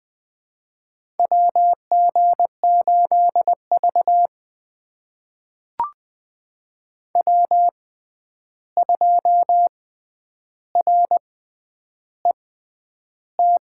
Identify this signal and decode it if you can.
Morse code, heard as a keyed tone, about 20 wpm, spaced as for 10 wpm, 700 Hz. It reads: WG8V W2RET
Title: USA Call Signs - 6x Character Spacing - 20wpm